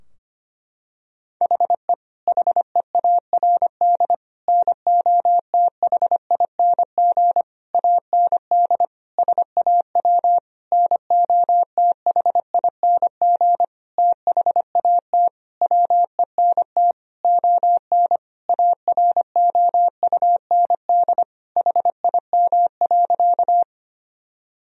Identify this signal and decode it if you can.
Morse code, heard as a keyed tone, about 25 wpm, 700 Hz. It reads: HE HEARD NOTHING AND SAW NOTHING THAT WENT ON AROUND HIM.
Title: Sentences from Top 1000 Words - Review Part 3 - 25wpm